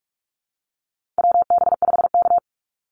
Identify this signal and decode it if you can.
Morse code, heard as a keyed tone, about 44 wpm, 700 Hz. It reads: WB5K
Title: Call Signs Repeated 2x - 45wpm